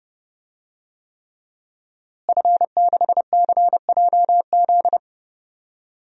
Morse code, heard as a keyed tone, about 31 wpm, 700 Hz. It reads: F6CJZ
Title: DXCC-100 Call Signs - 5 Characters - Repeated 3x - 30wpm